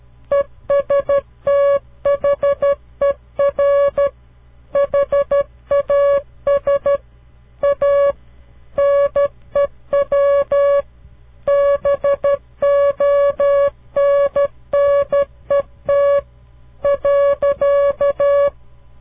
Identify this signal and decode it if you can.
Morse code, heard as a keyed tone, about 13 wpm, 560 Hz. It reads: ESTHER HAS A NEW BONNET .
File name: CW Learning Programs / farnsworth / unit 2